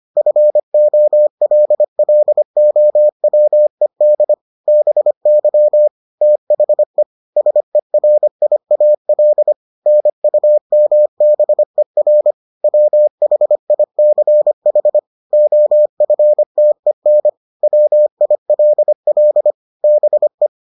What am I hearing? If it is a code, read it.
Morse code, heard as a keyed tone, about 25 wpm, 600 Hz. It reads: FOLLOWED BY THE SERIAL NUMBER WHICH OFTEN WILL BE